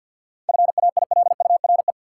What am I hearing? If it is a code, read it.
Morse code, heard as a keyed tone, about 46 wpm, 700 Hz. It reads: FAILURE
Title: Sentences from Top 900 Words - New Word Drill - 50wpm